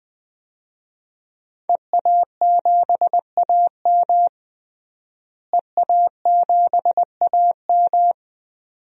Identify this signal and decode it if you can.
Morse code, heard as a keyed tone, about 20 wpm, 700 Hz. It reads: EA7AM EA7AM
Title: DXCC-100 Call Signs - 5 Characters - Repeated 2x - 20wpm